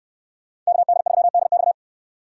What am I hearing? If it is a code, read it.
Morse code, heard as a keyed tone, about 68 wpm, 700 Hz. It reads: KD3DX